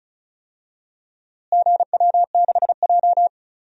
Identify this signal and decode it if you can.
Morse code, heard as a keyed tone, about 34 wpm, 700 Hz. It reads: GW6J
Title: DXCC-100 Call Signs - 4 Characters - Repeated 3x - 35wpm